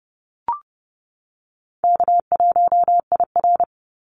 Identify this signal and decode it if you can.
Morse code, heard as a keyed tone, about 30 wpm, 700 Hz. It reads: K1IR
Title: Call Signs - 30wpm